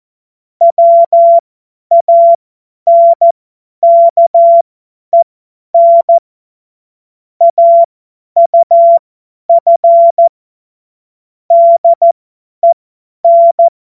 Morse code, heard as a keyed tone, about 14 wpm, spaced as for 10 wpm, 680 Hz. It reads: WANKEN AUF DEN
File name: Morsekurs / Witze / Witz 13. Juli 2021, 50 F70